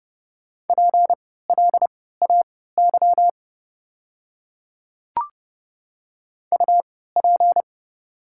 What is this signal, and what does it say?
Morse code, heard as a keyed tone, about 30 wpm, spaced as for 17 wpm, 700 Hz. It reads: PLAY UP
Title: 3x Character Spacing - Top 200 Words - 30wpm